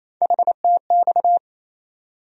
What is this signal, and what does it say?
Morse code, heard as a keyed tone, about 28 wpm, 700 Hz. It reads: HTX
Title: USA Call Sign Postfixes - 3 Letters 28wpm